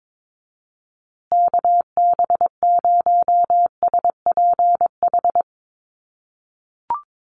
Morse code, heard as a keyed tone, about 22 wpm, 700 Hz. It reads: KB0SPH
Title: USA Call Signs Repeated 2x 22wpm